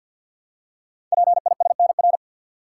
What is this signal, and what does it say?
Morse code, heard as a keyed tone, about 46 wpm, 700 Hz. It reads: PISNR
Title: Code Groups - 5 Letters - 50wpm